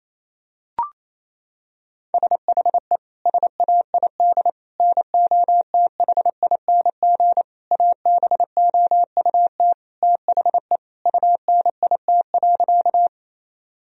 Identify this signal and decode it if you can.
Morse code, heard as a keyed tone, about 28 wpm, 700 Hz. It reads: SHE SAID NOTHING ABOUT THE UNIT.
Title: Sentences from Top 1000 Words - Review Part 2 - 28wpm